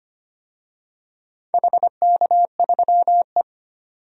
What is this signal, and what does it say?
Morse code, heard as a keyed tone, about 25 wpm, 700 Hz. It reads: HK3E
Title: DXCC-100 Call Signs - 4 Characters - 25wpm